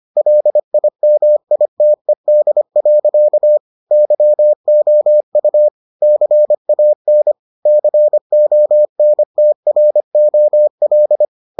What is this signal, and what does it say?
Morse code, heard as a keyed tone, about 25 wpm, 600 Hz. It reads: LIMITED. YOU CAN CONTROL